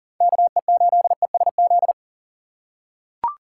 Morse code, heard as a keyed tone, about 40 wpm, 700 Hz. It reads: KE8ESZ
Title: USA Call Signs - 6x Character Spacing - 40wpm